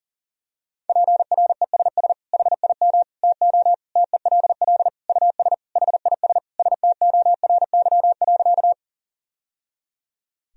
Morse code, heard as a keyed tone, about 40 wpm, 700 Hz. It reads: PRESS HIM TO TELL US HIS STORY.
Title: Sentences from Top 1000 Words - Review Part 1 - 40wpm